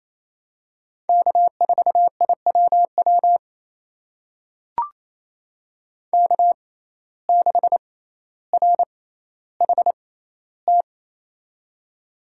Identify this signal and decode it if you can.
Morse code, heard as a keyed tone, about 28 wpm, spaced as for 14 wpm, 700 Hz. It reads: K4IWW K6RHT
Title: USA Call Signs - 5 Characters - 6x Character Spacing - 28wpm